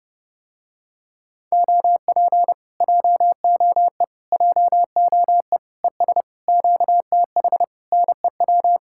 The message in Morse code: OP JOE JOE ES QTH NEW

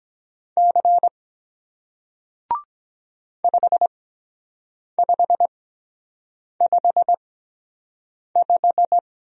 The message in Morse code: C 5555

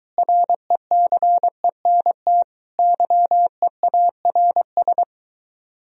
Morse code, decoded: RECENT YEARS

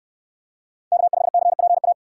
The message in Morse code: K5CPA

